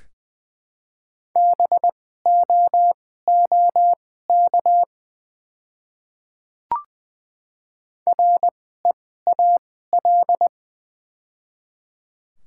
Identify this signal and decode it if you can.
Morse code, heard as a keyed tone, about 20 wpm, spaced as for 15 wpm, 700 Hz. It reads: BOOK REAL
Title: 2x Character Spacing - Top 300 Words - 20wpm